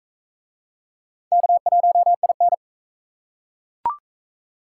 Morse code, heard as a keyed tone, about 41 wpm, 700 Hz. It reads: K1IN